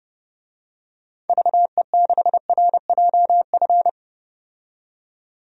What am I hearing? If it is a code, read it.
Morse code, heard as a keyed tone, about 30 wpm, 700 Hz. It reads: VE6RJF
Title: DXCC-100 Call Signs - 6 Characters - 30wpm